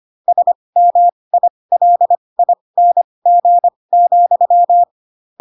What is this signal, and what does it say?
Morse code, heard as a keyed tone, about 25 wpm, spaced as for 20 wpm, 700 Hz. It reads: SMILING,